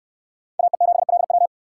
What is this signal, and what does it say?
Morse code, heard as a keyed tone, about 66 wpm, 700 Hz. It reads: AE7LW